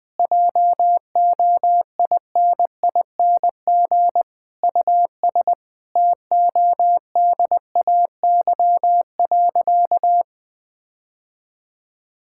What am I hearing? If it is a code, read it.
Morse code, heard as a keyed tone, about 20 wpm, 700 Hz. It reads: JOINING US TODAY.
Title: Sentences with 5-Word Phrases - Spoken Beginning - 20wpm